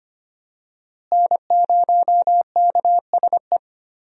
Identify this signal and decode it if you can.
Morse code, heard as a keyed tone, about 25 wpm, 700 Hz. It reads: N0KSE
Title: USA Call Signs - 5 Characters - Repeated 2x - 25wpm